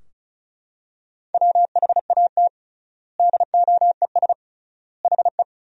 Morse code, heard as a keyed tone, about 36 wpm, 700 Hz. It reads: WHAT DOES HE
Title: 3x Word-Spacing - Sentences from Top 400 Words - 35wpm